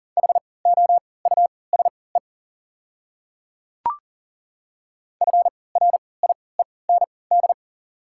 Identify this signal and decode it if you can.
Morse code, heard as a keyed tone, about 39 wpm, spaced as for 22 wpm, 700 Hz. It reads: HOUSE FRIEND
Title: 3x Character Spacing - Top 300 Words - 40wpm